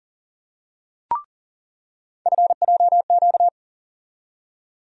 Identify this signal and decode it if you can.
Morse code, heard as a keyed tone, about 40 wpm, 700 Hz. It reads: FJQ
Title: USA Call Sign Postfixes - 3 Letters 40wpm